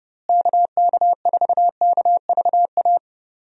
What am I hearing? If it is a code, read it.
Morse code, heard as a keyed tone, about 32 wpm, 700 Hz. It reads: KK4KVA
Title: USA Call Signs - 6 Characters - Repeated 2x - 30wpm